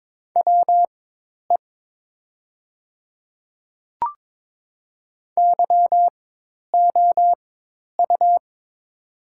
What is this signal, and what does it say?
Morse code, heard as a keyed tone, about 22 wpm, spaced as for 10 wpm, 700 Hz. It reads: WE YOU
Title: 4x Character Spacing - Top 100 Words - 22wpm